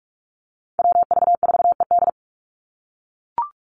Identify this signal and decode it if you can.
Morse code, heard as a keyed tone, about 44 wpm, 700 Hz. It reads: WV4ED
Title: USA Call Signs 45wpm